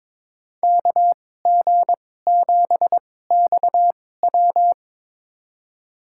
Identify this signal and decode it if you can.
Morse code, heard as a keyed tone, about 22 wpm, spaced as for 16 wpm, 700 Hz. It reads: KG7XW
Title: USA Call Signs - 5 Characters - 2x Character Spacing - 22wpm